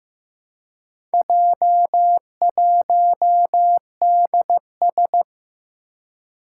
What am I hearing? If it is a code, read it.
Morse code, heard as a keyed tone, about 15 wpm, 700 Hz. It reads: J1DS
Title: Four Challenging Letter-Number Combinations 15wpm